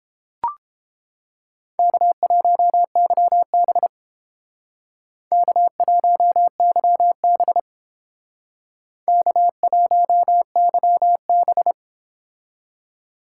Morse code, heard as a keyed tone, about 31 wpm, 700 Hz. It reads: K1YB K1YB K1YB